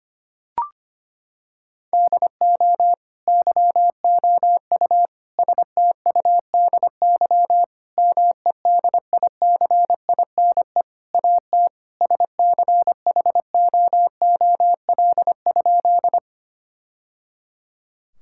Morse code, heard as a keyed tone, about 25 wpm, 700 Hz. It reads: DO YOU STUDY MEDICINE AT SCHOOL?
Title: Sentences from Top 1000 Words - Review Part 6 - 25wpm